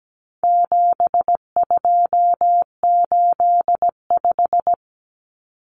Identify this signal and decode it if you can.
Morse code, heard as a keyed tone, about 17 wpm, 700 Hz. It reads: 7285
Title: Common Cut Numbers - 4-Digit 17wpm